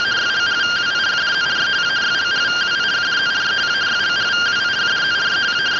Two-tone teleprinter signal, about 170 Hz shift.